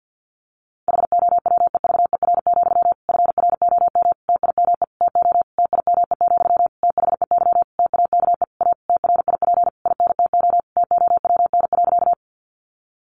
Morse code, heard as a keyed tone, about 50 wpm, 700 Hz. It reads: HOWEVER, FROM TIME TO TIME, THEY TAKE A TRIP INTO TOWN.
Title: Sentences from Top 1300 Words 50wpm